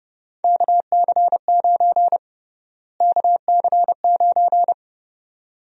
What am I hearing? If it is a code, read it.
Morse code, heard as a keyed tone, about 32 wpm, 700 Hz. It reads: KC9 KC9
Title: USA Call Sign Prefixes - Repeated 2x - 30wpm